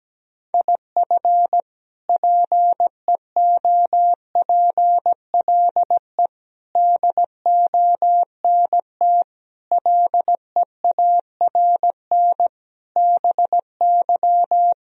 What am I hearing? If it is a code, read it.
Morse code, heard as a keyed tone, about 17 wpm, 700 Hz. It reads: IF PEOPLE DONT LEARN BY
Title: Sentences from Top 1000 Words - Review Part 3 - 17wpm